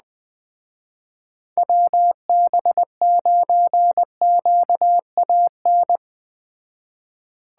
Morse code, heard as a keyed tone, about 21 wpm, 700 Hz. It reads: WB9QAN